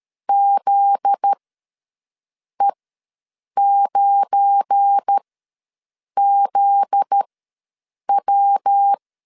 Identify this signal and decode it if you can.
Morse code, heard as a keyed tone, about 13 wpm, spaced as for 6 wpm, 800 Hz. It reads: ZE9ZW
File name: LEZIONI / 7 GF9Z / 7 GF9Z ws 12wpm 3321